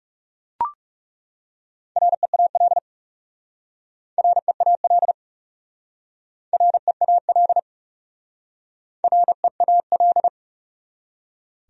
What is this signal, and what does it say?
Morse code, heard as a keyed tone, about 37 wpm, 700 Hz. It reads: REAL REAL REAL REAL
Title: Speed-Racing Top 500 Words 30wpm